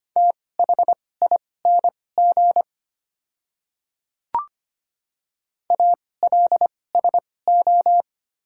Morse code, heard as a keyed tone, about 25 wpm, spaced as for 18 wpm, 700 Hz. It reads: THING ALSO
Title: 2x Character Spacing - Top 200 Words - 25wpm